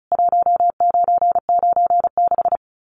Morse code, heard as a keyed tone, about 34 wpm, 700 Hz. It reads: 1996
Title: Common QSO Phrases 35wpm